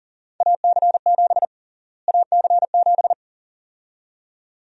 Morse code, heard as a keyed tone, about 40 wpm, 700 Hz. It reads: AC7 AC7
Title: USA Call Sign Prefixes - Repeated 2x - 40wpm